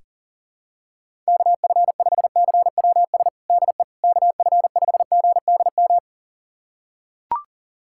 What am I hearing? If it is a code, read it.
Morse code, heard as a keyed tone, about 40 wpm, 700 Hz. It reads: KF5CWS DE KF5GDM
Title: USA Callsign DE Callsign - 6 Characters - 40wpm